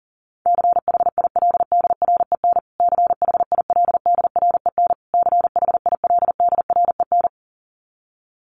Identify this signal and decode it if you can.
Morse code, heard as a keyed tone, about 37 wpm, 700 Hz. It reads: CHILDREN CHILDREN CHILDREN
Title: Top 500 Words Repeated 40wpm